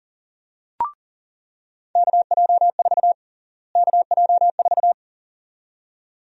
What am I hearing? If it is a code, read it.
Morse code, heard as a keyed tone, about 40 wpm, 700 Hz. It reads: KJ4 KJ4
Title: USA Call Sign Prefixes - Repeated 2x - 40wpm